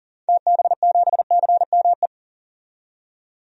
Morse code, heard as a keyed tone, about 40 wpm, 700 Hz. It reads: TB7CME
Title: DXCC-100 Call Signs - 6 Characters - 40wpm